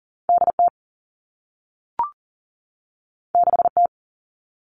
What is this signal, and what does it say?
Morse code, heard as a keyed tone, about 41 wpm, 700 Hz. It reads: D T 6 T
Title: Two Challenging Letter-Number Combinations 40wpm